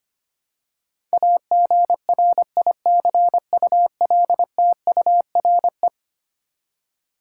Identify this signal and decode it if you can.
Morse code, heard as a keyed tone, about 25 wpm, 700 Hz. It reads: AGRICULTURE
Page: morse-code-ninja-practice-25wpm